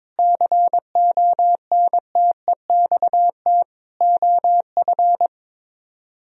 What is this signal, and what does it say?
Morse code, heard as a keyed tone, about 22 wpm, 700 Hz. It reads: CONTEXT OF
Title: Speed-Racing - Sets of 4 Words 22wpm